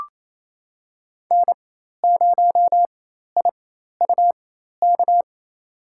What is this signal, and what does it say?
Morse code, heard as a keyed tone, about 28 wpm, spaced as for 13 wpm, 700 Hz. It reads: N0IUK
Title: USA Call Signs - 5 Characters - 4x Character Spacing - 28wpm